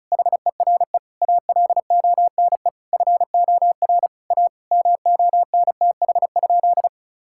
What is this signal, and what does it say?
Morse code, heard as a keyed tone, about 34 wpm, 700 Hz. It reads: HERE ALONE FOR A MONTH?
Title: Sentences from Top 1000 Words - Review Part 5 - 35wpm